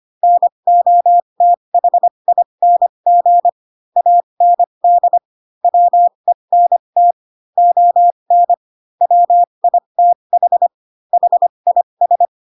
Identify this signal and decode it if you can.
Morse code, heard as a keyed tone, about 25 wpm, spaced as for 22 wpm, 700 Hz. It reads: NOTHING AND WENT ON WITH HIS
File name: Pinocchio / Pinocchio ch3 18 25 ik1hge